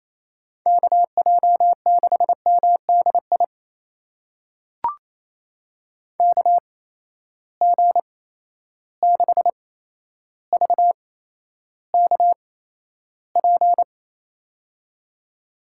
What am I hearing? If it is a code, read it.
Morse code, heard as a keyed tone, about 28 wpm, spaced as for 12 wpm, 700 Hz. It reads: KJ6MDI KG6VKP
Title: USA Call Signs - 6 Characters - 8x Character Spacing - 28wpm